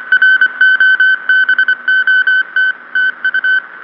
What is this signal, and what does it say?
Morse code, heard as a keyed tone, about 25 wpm, 1500 Hz. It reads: ROBOT TU